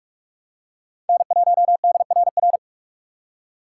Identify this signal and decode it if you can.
Morse code, heard as a keyed tone, about 46 wpm, 700 Hz. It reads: N1DRR
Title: USA Call Signs - 5 Characters - 6x Character Spacing - 45wpm